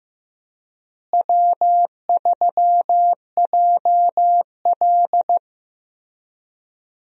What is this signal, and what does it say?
Morse code, heard as a keyed tone, about 15 wpm, 700 Hz. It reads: W3JL
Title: USA Call Signs - 4 Characters - Repeated 3x - 15wpm